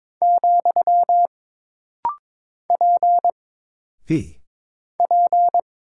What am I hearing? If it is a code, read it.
Morse code, heard as a keyed tone, about 22 wpm, 700 Hz. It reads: , P P
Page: This rapid-fire timing is geared to push your ICR (Instant Character Recognition) skills.